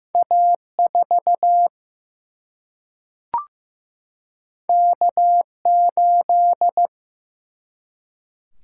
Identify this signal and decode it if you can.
Morse code, heard as a keyed tone, about 16 wpm, 700 Hz. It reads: A 4 K 8
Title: Single-Letter Single-Number 15wpm